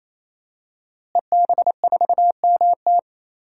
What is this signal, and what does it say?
Morse code, heard as a keyed tone, about 28 wpm, 700 Hz. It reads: EB4MT
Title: DXCC-100 Call Signs - 5 Characters - Repeated 3x - 28wpm